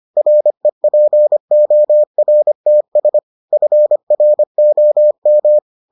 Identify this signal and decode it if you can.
Morse code, heard as a keyed tone, about 25 wpm, 600 Hz. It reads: REPORTS FROM